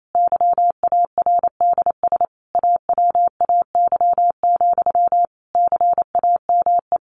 Morse code, heard as a keyed tone, about 28 wpm, 700 Hz. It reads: YARDS AWAY, CAME
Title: A Sound Of Thunder 28wpm